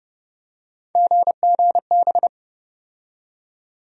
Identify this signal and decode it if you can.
Morse code, heard as a keyed tone, about 30 wpm, 700 Hz. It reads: GGB
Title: USA Call Sign Postfixes - 3 Letters 30wpm